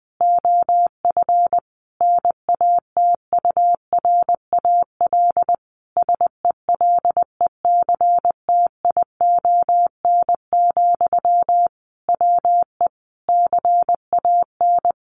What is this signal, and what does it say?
Morse code, heard as a keyed tone, about 20 wpm, 700 Hz. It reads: OF NATURAL SELECTION, WE CAN
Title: Sentences from Top 1400 Words 20wpm